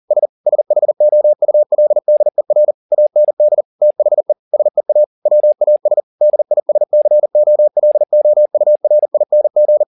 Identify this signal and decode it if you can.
Morse code, heard as a keyed tone, about 40 wpm, 600 Hz. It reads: S SHOULDER AND THE SEA WAS DISCOLOURING
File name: the old man and the sea 2 40wpm